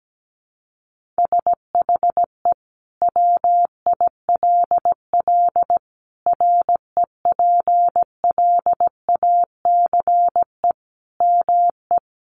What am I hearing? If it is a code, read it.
Morse code, heard as a keyed tone, about 17 wpm, 700 Hz. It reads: SHE WILL REPLACE ME
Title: Sentences from Top 1400 Words 17wpm